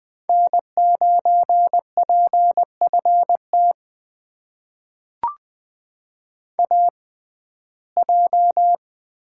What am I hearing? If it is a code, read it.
Morse code, heard as a keyed tone, about 20 wpm, spaced as for 14 wpm, 700 Hz. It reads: N9PFT AJ
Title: USA Call Signs - 5 Characters - 6x Character Spacing - 20wpm